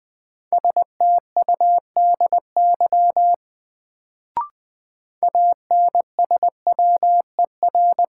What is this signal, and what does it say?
Morse code, heard as a keyed tone, about 21 wpm, 700 Hz. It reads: STUDY ANSWER
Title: Sentences from Top 300 Words - New Word Drill - 20wpm